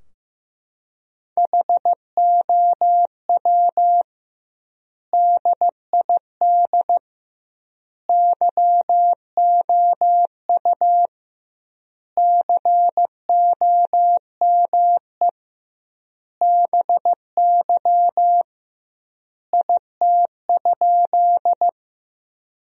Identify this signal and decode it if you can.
Morse code, heard as a keyed tone, about 15 wpm, 700 Hz. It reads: HOW DID YOU COME BY IT?
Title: 2x Word-Spacing - Sentences from Top 100 Words - 15wpm